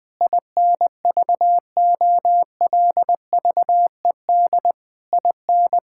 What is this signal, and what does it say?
Morse code, heard as a keyed tone, about 20 wpm, 700 Hz. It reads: INVOLVED IN
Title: Sentences from Top 1000 Words - Review Part 5 - 20wpm